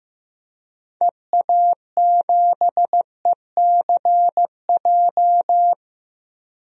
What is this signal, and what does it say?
Morse code, heard as a keyed tone, about 15 wpm, 700 Hz. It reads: EA7ECJ
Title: DXCC-100 Call Signs - 6 Characters - Repeated 3x - 15wpm